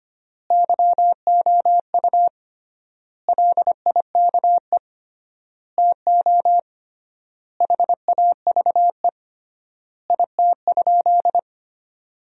Morse code, heard as a keyed tone, about 25 wpm, 700 Hz. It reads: YOU LIKE TO HAVE IT?
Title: 3x Word-Spacing - Sentences from Top 100 Words - 25wpm